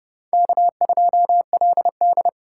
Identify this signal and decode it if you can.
Morse code, heard as a keyed tone, about 30 wpm, 700 Hz. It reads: K2LD
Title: USA Call Signs - 4 Characters - 3x Character Spacing - 30wpm